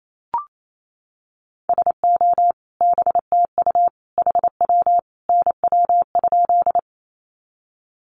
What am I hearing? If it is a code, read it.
Morse code, heard as a keyed tone, about 28 wpm, 700 Hz. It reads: SO BTU HW NW?
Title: Common QSO Phrases 28wpm